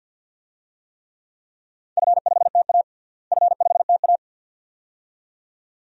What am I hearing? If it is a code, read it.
Morse code, heard as a keyed tone, about 47 wpm, 700 Hz. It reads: F5TA F5TA
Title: DXCC-100 Call Signs - 4 Characters - Repeated 2x - 50wpm